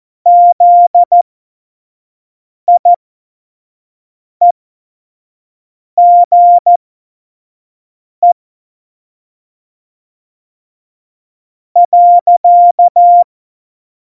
Morse code, heard as a keyed tone, about 14 wpm, spaced as for 4 wpm, 700 Hz. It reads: ZIEGE.
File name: Morsekurs / Witze / Witz 11. Juni 2019, 25 F70